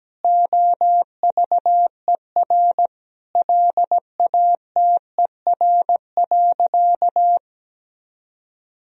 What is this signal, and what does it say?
Morse code, heard as a keyed tone, about 17 wpm, 700 Hz. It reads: OVER LATER.
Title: Sentences from Top 1000 Words - Review Part 4 - 17wpm